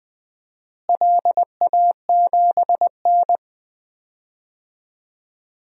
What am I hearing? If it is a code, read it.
Morse code, heard as a keyed tone, about 20 wpm, 700 Hz. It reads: LA7N
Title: DXCC-100 Call Signs - 4 Characters - Repeated 3x - 20wpm